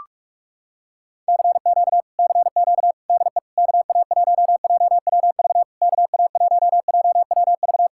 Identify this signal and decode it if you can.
Morse code, heard as a keyed tone, about 37 wpm, 700 Hz. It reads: CQ CQ DE KA1JWV KA1JWV
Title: Speed-Racing Common QSO Phrases - 30wpm